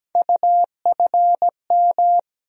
Morse code, heard as a keyed tone, about 17 wpm, 700 Hz. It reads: UFM
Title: USA Call Sign Postfixes - 3 Letters 17wpm